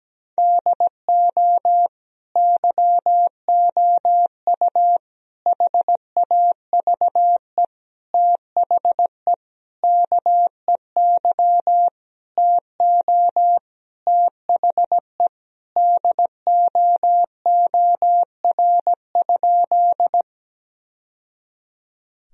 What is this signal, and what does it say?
Morse code, heard as a keyed tone, about 17 wpm, 700 Hz. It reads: DO YOU HAVE THE KEY TO THE DOOR?
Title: Sentences from Top 1000 Words - Review Part 4 - 17wpm